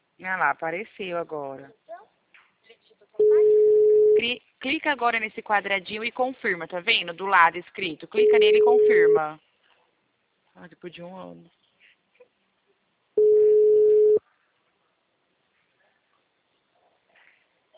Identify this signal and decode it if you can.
Ringback tone (Europe, 425 Hz), with speech.